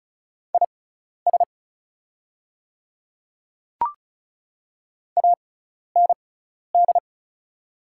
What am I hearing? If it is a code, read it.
Morse code, heard as a keyed tone, about 36 wpm, spaced as for 12 wpm, 700 Hz. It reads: IS AND
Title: 6x Character Spacing - Top 100 Words - 35wpm